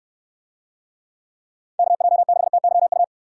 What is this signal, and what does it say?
Morse code, heard as a keyed tone, about 63 wpm, 700 Hz. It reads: KJ6TYU